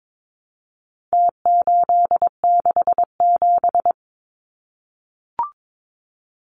Morse code, heard as a keyed tone, about 22 wpm, 700 Hz. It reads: T867
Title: Numbers - Contest Style - 4-Digit 22wpm